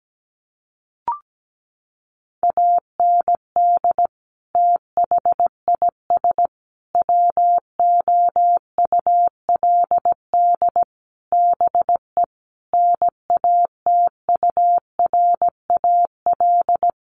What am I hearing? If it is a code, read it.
Morse code, heard as a keyed tone, about 17 wpm, 700 Hz. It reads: AND THIS WOULD BE NATURAL
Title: Sentences from Top 1400 Words 17wpm